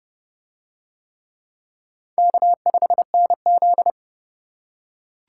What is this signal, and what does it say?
Morse code, heard as a keyed tone, about 28 wpm, 700 Hz. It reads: K5NZ